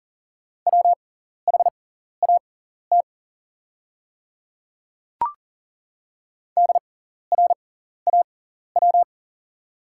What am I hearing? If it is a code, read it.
Morse code, heard as a keyed tone, about 40 wpm, spaced as for 14 wpm, 690 Hz. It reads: WHAT DRAW
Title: 6x Character Spacing - Top 500 Words - 40wpm